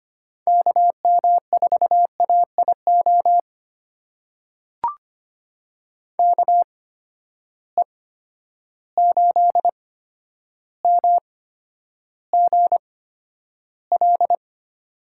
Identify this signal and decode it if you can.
Morse code, heard as a keyed tone, about 25 wpm, spaced as for 11 wpm, 700 Hz. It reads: KM4AIO KE8MGL
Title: USA Call Signs - 6 Characters - 8x Character Spacing - 25wpm